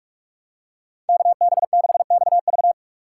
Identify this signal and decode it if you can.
Morse code, heard as a keyed tone, about 42 wpm, 700 Hz. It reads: KB6XV
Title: USA Call Signs - 5 Characters - 6x Character Spacing - 45wpm